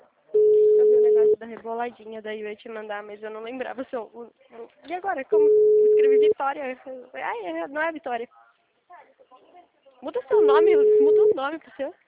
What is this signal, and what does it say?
Ringback tone (Europe, 425 Hz), with speech.
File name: out- -2010- - - .28000